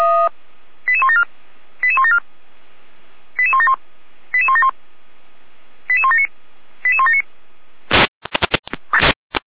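Five-tone selective call (ZVEI), heard as quick runs of five tones, tones of about 70 ms. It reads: 88174 88174 88171 88171 88179 88179
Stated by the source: ZVEI five-tone selective call 88413 received via SDR.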